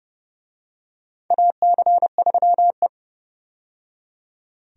Morse code, heard as a keyed tone, about 29 wpm, 700 Hz. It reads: AC3E
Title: USA Call Signs - 4 Characters - 30wpm